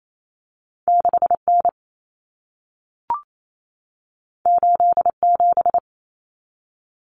Morse code, heard as a keyed tone, about 28 wpm, 700 Hz. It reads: 6N 87
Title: Common Cut Numbers - 2-Digit 28wpm